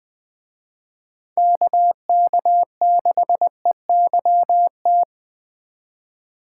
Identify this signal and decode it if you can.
Morse code, heard as a keyed tone, about 20 wpm, 700 Hz. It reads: KK6EYT
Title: Speed Racing - USA 2x3 Call Signs 17wpm